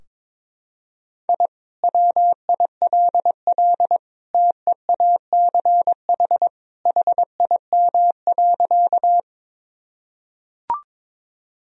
Morse code, heard as a keyed tone, about 22 wpm, 700 Hz. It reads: I WILL TEACH HIM.
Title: Sentences from Top 1000 Words - Review Part 2 - 22wpm